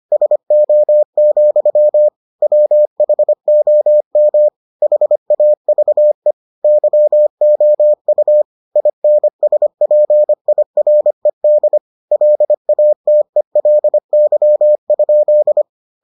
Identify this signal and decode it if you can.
Morse code, heard as a keyed tone, about 25 wpm, 600 Hz. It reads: SO, WHOM HAVE YOU INSPIRED LATELY?